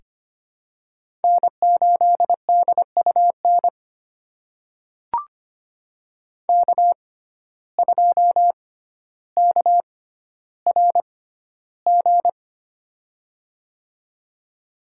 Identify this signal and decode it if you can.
Morse code, heard as a keyed tone, about 25 wpm, spaced as for 13 wpm, 700 Hz. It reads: N8DUN K2KRG
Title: USA Call Signs - 5 Characters - 6x Character Spacing - 25wpm